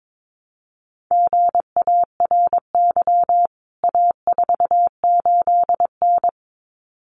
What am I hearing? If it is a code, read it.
Morse code, heard as a keyed tone, about 22 wpm, 700 Hz. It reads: GARY A48N
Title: CWT Exchange 22wpm